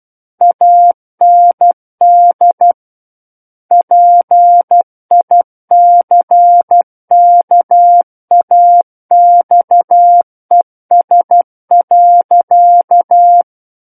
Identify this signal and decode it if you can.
Morse code, heard as a keyed tone, about 12 wpm, 700 Hz. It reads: AND PICKAXES.
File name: Book One / 03-Chapter 3 - 12 WPM